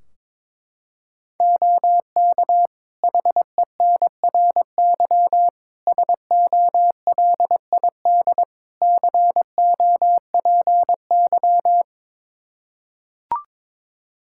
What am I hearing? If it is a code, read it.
Morse code, heard as a keyed tone, about 22 wpm, 700 Hz. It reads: OK HENRY SOLID COPY
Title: Practice QSO - Exchange 3 - More Info - 22wpm